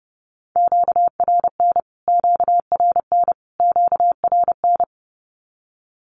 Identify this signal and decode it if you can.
Morse code, heard as a keyed tone, about 30 wpm, 700 Hz. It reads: QRN QRN QRN
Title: Q-Codes Repeated 3x 30wpm